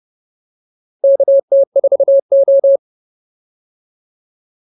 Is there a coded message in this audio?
Yes: Morse code from 1.0 to 2.8 s, heard as a keyed tone, about 30 wpm, 550 Hz. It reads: KT4O